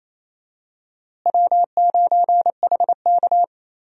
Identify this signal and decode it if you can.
Morse code, heard as a keyed tone, about 27 wpm, 700 Hz. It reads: W9HK
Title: USA Call Signs - 4 Characters - Repeated 2x - 28wpm